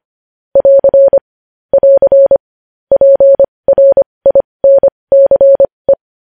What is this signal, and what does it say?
Morse code, heard as a keyed tone, about 25 wpm, spaced as for 20 wpm, 550 Hz. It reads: <AR> <AR> PRINCE
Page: News Headlines in Morse Code at 20 WPM